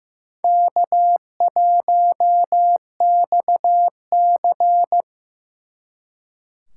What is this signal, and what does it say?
Morse code, heard as a keyed tone, about 18 wpm, 700 Hz. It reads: K1XC
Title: Speed Racing - USA 1x2 Call Signs 15wpm